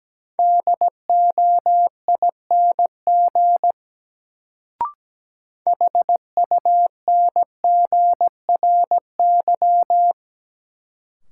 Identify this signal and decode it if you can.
Morse code, heard as a keyed tone, about 17 wpm, 700 Hz. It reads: DOING HUNGRY
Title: Sentences from Top 1400 Words - New Word Drill - 17wpm